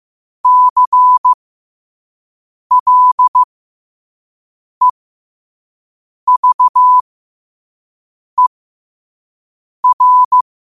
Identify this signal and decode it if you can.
Morse code, heard as a keyed tone, about 15 wpm, spaced as for 5 wpm, 1000 Hz. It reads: CLEVER
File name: 5wpm